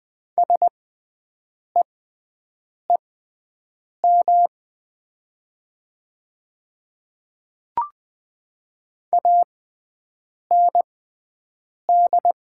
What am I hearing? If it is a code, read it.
Morse code, heard as a keyed tone, about 20 wpm, spaced as for 7 wpm, 700 Hz. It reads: SEEM AND
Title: 6x Character Spacing - Top 300 Words - 20wpm